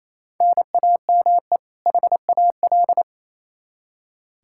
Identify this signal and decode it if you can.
Morse code, heard as a keyed tone, about 28 wpm, 700 Hz. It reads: NAME HAL
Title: Practice QSO - Exchange 1 - Beginning - 28wpm